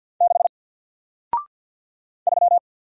Morse code, heard as a keyed tone, about 50 wpm, 700 Hz. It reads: 6 3
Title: Single Letter-Number Alt 50wpm